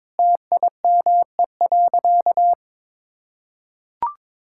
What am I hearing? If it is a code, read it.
Morse code, heard as a keyed tone, about 22 wpm, 700 Hz. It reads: TIME.
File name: practice / 22wpm / Sentences with 3-Word Phrases - 22wpm